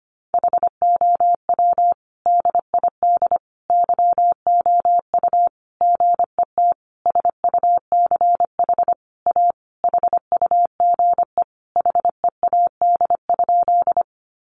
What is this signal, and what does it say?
Morse code, heard as a keyed tone, about 25 wpm, 700 Hz. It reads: HOW DID YOU GET SUCH A HUGE HEAD?
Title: Sentences from Top 1100 Words 25wpm